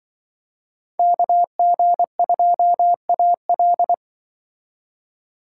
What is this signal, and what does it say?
Morse code, heard as a keyed tone, about 25 wpm, 700 Hz. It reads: KG2AL
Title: Speed Racing - USA 2x2 Call Signs 20wpm